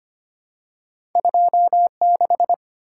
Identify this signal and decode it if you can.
Morse code, heard as a keyed tone, about 25 wpm, 700 Hz. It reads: 26
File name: Two Letter-Number Combinations - 25wpm